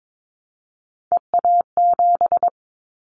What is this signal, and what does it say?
Morse code, heard as a keyed tone, about 22 wpm, 700 Hz. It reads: EA7
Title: Callsign Prefixes for Top 25 DXCC - Repeated 3x - 22wpm